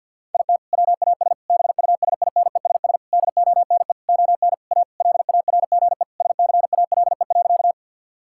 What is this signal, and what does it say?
Morse code, heard as a keyed tone, about 50 wpm, 700 Hz. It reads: IT WAS BUSINESS DONE ON A LARGE SCALE.
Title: Sentences from Top 1000 Words - Review Part 5 - 50wpm